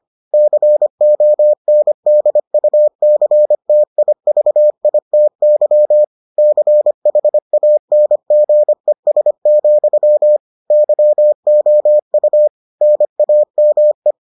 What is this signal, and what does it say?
Morse code, heard as a keyed tone, about 25 wpm, 600 Hz. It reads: CONDUCTIVITY CHANGES, YOU NAME